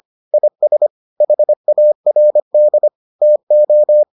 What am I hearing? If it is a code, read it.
Morse code, heard as a keyed tone, about 25 wpm, 600 Hz. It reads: IS HARD TO